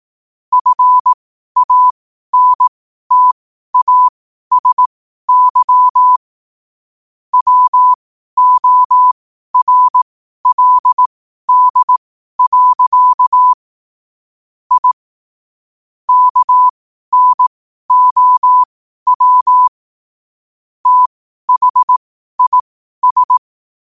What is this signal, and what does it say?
Morse code, heard as a keyed tone, about 18 wpm, spaced as for 12 wpm, 1000 Hz. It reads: FANTASY WORLD. I KNOW THIS
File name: qotdfiles / 12wpm